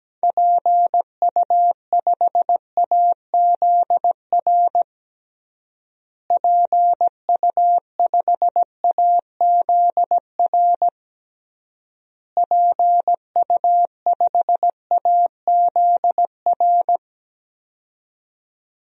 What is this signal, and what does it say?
Morse code, heard as a keyed tone, about 17 wpm, 700 Hz. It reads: PU5AZR PU5AZR PU5AZR
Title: DXCC-100 Call Signs - 6 Characters - Repeated 3x - 17wpm